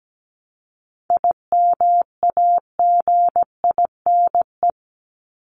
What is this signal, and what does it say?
Morse code, heard as a keyed tone, about 17 wpm, 700 Hz. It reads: IMAGINE
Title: Top 1251-1500 Words 17wpm